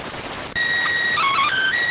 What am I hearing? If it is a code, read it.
SSTV (slow-scan television), PD 290.